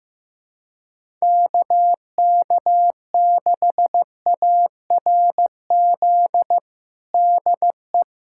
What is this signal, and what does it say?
Morse code, heard as a keyed tone, about 15 wpm, 700 Hz. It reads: KK6ARZ DE
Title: USA Callsign DE Callsign 15wpm